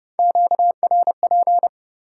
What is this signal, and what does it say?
Morse code, heard as a keyed tone, about 30 wpm, 700 Hz. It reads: QRP
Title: Top 90 QSO Elements 30wpm